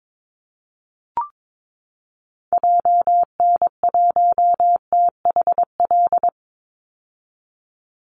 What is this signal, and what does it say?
Morse code, heard as a keyed tone, about 22 wpm, 700 Hz. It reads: JN1THL
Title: Call Signs - 22wpm